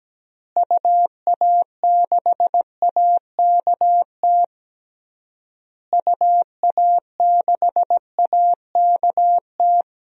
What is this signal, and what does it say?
Morse code, heard as a keyed tone, about 17 wpm, 700 Hz. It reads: UA6AKT UA6AKT
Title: DXCC-100 Call Signs - 6 Characters - Repeated 3x - 17wpm